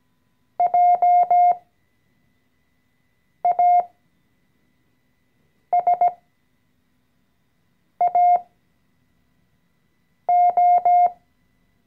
Morse code, heard as a keyed tone, about 17 wpm, spaced as for 4 wpm, 710 Hz. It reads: JASAO
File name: CW-Online-Kurs / Anfänger / Lektion 014